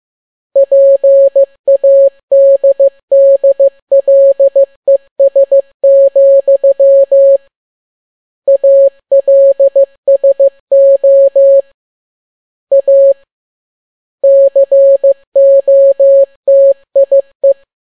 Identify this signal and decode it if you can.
Morse code, heard as a keyed tone, about 15 wpm, 550 Hz. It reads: PADDLES, ALSO A COTIE
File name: n6tt p1 15C13